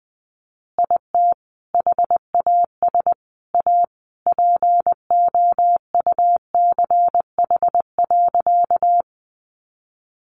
Morse code, heard as a keyed tone, about 20 wpm, 700 Hz. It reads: IT HAS A POUCH.
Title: Easy Phrases 20wpm